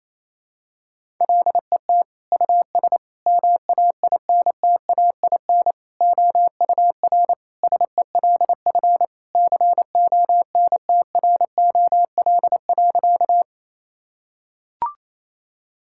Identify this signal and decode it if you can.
Morse code, heard as a keyed tone, about 28 wpm, 700 Hz. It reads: LET US MAINTAIN OUR SELF CONTROL.
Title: Sentences from Top 1000 Words - Review Part 5 - 28wpm